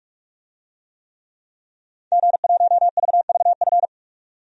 Morse code, heard as a keyed tone, about 46 wpm, 700 Hz. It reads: G1VVF
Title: DXCC-100 Call Signs - 5 Characters - 45wpm